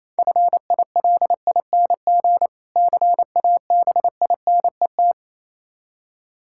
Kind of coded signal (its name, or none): Morse code